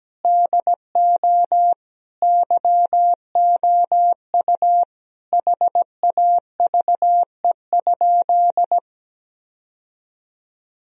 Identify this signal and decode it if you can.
Morse code, heard as a keyed tone, about 17 wpm, 700 Hz. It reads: DO YOU HAVE?
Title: Sentences from Top 1000 Words - Review Part 3 - 17wpm